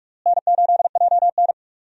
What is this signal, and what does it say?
Morse code, heard as a keyed tone, about 41 wpm, 700 Hz. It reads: N8JN
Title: USA Call Signs - 4 Characters - Repeated 2x - 45wpm